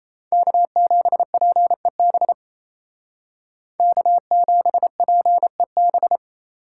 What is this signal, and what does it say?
Morse code, heard as a keyed tone, about 37 wpm, 700 Hz. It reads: K7PEB K7PEB